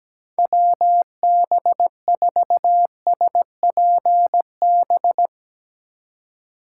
Morse code, heard as a keyed tone, about 17 wpm, 700 Hz. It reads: WB4SPB
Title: K1USN SST QSO Practice - 17wpm